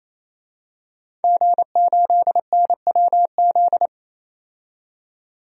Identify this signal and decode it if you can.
Morse code, heard as a keyed tone, about 28 wpm, 700 Hz. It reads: G8NWZ
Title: DXCC-100 Call Signs - 5 Characters - 28wpm